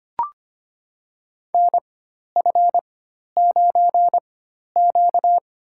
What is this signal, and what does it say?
Morse code, heard as a keyed tone, about 25 wpm, spaced as for 12 wpm, 700 Hz. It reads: NF9Q
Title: USA Call Signs - 4x Character Spacing - 25wpm